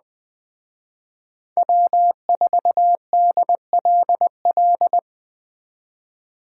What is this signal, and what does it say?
Morse code, heard as a keyed tone, about 20 wpm, 700 Hz. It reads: W4DLL